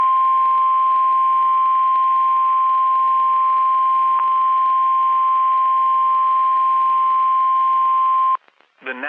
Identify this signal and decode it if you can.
Emergency Alert System (SAME) tones (the 1050 Hz attention tone), with speech. It is heard from 0.4 to 8.3 s.